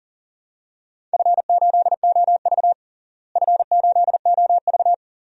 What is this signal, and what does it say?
Morse code, heard as a keyed tone, about 40 wpm, 700 Hz. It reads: F8OV F8OV
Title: DXCC-100 Call Signs - 4 Characters - Repeated 3x - 40wpm